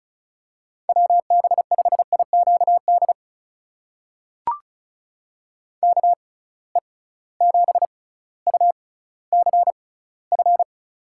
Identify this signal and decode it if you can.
Morse code, heard as a keyed tone, about 36 wpm, spaced as for 18 wpm, 700 Hz. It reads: WB5IQD KE7UCF
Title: USA Call Signs - 6 Characters - 6x Character Spacing - 35wpm